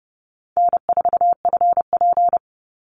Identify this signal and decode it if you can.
Morse code, heard as a keyed tone, about 30 wpm, 700 Hz. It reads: N4FP
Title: Easy Call Signs Repeated 2x - 30wpm